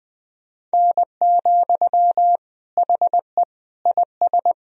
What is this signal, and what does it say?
Morse code, heard as a keyed tone, about 20 wpm, 700 Hz. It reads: N, HE IS
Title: Sentences from Top 400 Words and Abbreviations 20wpm